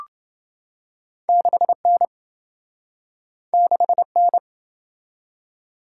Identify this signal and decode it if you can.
Morse code, heard as a keyed tone, about 27 wpm, 700 Hz. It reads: 6N 6N